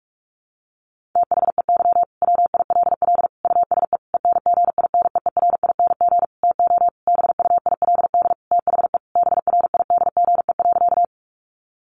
Morse code, heard as a keyed tone, about 44 wpm, 700 Hz. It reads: THEY WILL USE ENGINEERING TO BUILD THE BRIDGE.
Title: Sentences from Top 1500 Words 45wpm